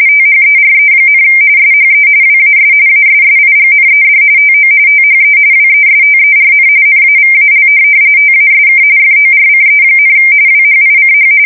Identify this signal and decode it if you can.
RTTY, about 45.45 baud. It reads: THERE, SHE WAS ASSIGNED TO THE WESTERN NAVAL TASK FORCE, WHICH
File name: fsk170 / USS-Massachusetts